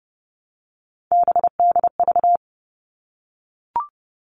Morse code, heard as a keyed tone, about 30 wpm, 700 Hz. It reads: BDV
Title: All Cut Numbers - 3-Digit 30wpm